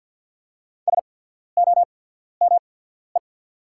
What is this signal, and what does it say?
Morse code, heard as a keyed tone, about 48 wpm, spaced as for 14 wpm, 690 Hz. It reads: SOME
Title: 8x Character Spacing - Top 200 Words - 50wpm